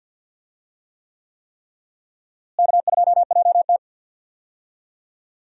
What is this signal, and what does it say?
Morse code, heard as a keyed tone, about 46 wpm, 700 Hz. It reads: K2JT
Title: USA Call Signs - 4 Characters - Repeated 3x - 50wpm